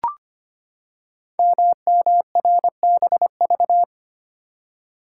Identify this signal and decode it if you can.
Morse code, heard as a keyed tone, about 25 wpm, 700 Hz. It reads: MMRBV
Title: Code Groups - 5 Letters - 25wpm